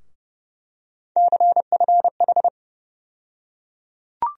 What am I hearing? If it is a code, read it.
Morse code, heard as a keyed tone, about 30 wpm, 700 Hz. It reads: CFH
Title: USA Call Sign Postfixes - 3 Letters 30wpm